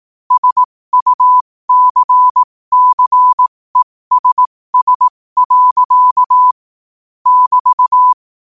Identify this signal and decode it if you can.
Morse code, heard as a keyed tone, about 18 wpm, spaced as for 15 wpm, 1000 Hz. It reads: SUCCESS. <BT>
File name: qotdfiles / 15wpm